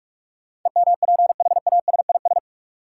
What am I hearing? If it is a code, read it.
Morse code, heard as a keyed tone, about 41 wpm, 700 Hz. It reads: EMPHASIS
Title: Sentences from Top 1400 Words - New Word Drill Alt - 45wpm